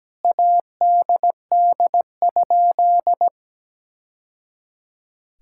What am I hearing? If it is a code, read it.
Morse code, heard as a keyed tone, about 17 wpm, 700 Hz. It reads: ADD?
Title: Sentences from Top 1000 Words - Review Part 5 - 17wpm